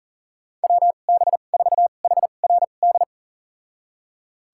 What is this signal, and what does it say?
Morse code, heard as a keyed tone, about 40 wpm, spaced as for 29 wpm, 700 Hz. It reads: WB4HRD
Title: USA Call Signs - 2x Character Spacing - 40wpm